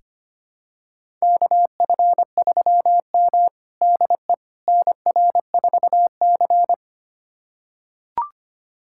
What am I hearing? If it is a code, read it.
Morse code, heard as a keyed tone, about 25 wpm, 700 Hz. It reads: KF3M DE NR4C
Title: USA Callsign DE Callsign - 4 Characters - 25wpm